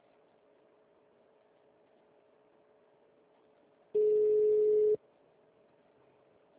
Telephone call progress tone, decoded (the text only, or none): ringback tone